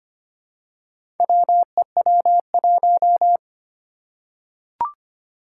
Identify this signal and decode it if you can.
Morse code, heard as a keyed tone, about 25 wpm, 700 Hz. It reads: WEW1
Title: Four Challenging Letter-Number Combinations 25wpm